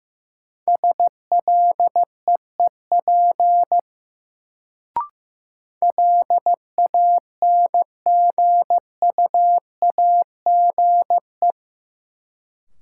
Morse code, heard as a keyed tone, about 15 wpm, 700 Hz. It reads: SLEEP LANGUAGE
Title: Sentences from Top 500 Words - New Word Drill - 15wpm